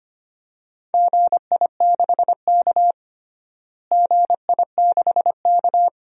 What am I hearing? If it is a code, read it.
Morse code, heard as a keyed tone, about 25 wpm, 700 Hz. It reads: GI6K GI6K
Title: DXCC-100 Call Signs - 4 Characters - Repeated 2x - 25wpm